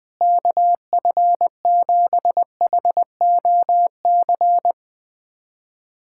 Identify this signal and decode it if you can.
Morse code, heard as a keyed tone, about 21 wpm, 700 Hz. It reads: KF7HOC
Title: USA Callsign DE Callsign 20wpm